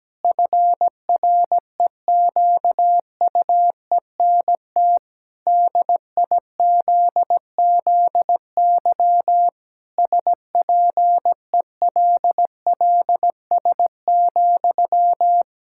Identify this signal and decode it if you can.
Morse code, heard as a keyed tone, about 17 wpm, 700 Hz. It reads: FREQUENT DIZZY SPELLS,
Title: Pangram Challenge v2 - 17wpm